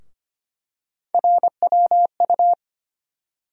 Morse code, heard as a keyed tone, about 25 wpm, 700 Hz. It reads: RWU
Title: USA Call Sign Postfixes - 3 Letters 25wpm